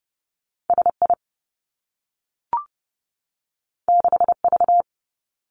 Morse code, heard as a keyed tone, about 30 wpm, 700 Hz. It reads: S I 6 V
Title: Two Challenging Letter-Number Combinations 30wpm